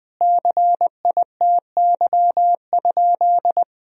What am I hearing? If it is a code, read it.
Morse code, heard as a keyed tone, about 20 wpm, 700 Hz. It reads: CITY?
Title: Sentences from Top 1000 Words - Review Part 1 - 20wpm